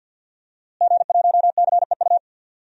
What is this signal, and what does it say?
Morse code, heard as a keyed tone, about 47 wpm, 700 Hz. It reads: G1CEU
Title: DXCC-100 Call Signs - 5 Characters - Repeated 2x - 50wpm